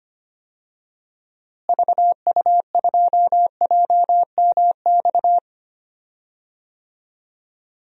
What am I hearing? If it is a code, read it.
Morse code, heard as a keyed tone, about 25 wpm, 700 Hz. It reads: VU2JMX
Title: DXCC-100 Call Signs - 6 Characters - Repeated 3x - 25wpm